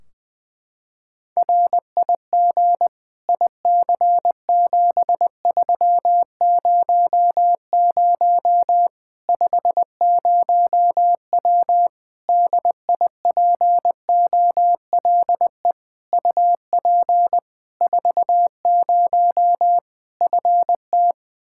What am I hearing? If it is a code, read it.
Morse code, heard as a keyed tone, about 20 wpm, 700 Hz. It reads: RIG IC7300 50W DIPOLE UP 40 FT
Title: Practice QSO - Exchange 2 - Info - 20wpm